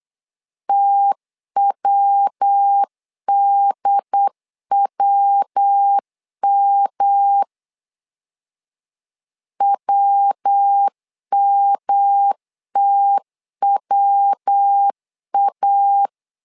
Morse code, heard as a keyed tone, about 8 wpm, 800 Hz. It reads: TWDWM WMTWA
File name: LEZIONI / 3 MWRO / 3 MW 8wpm 2501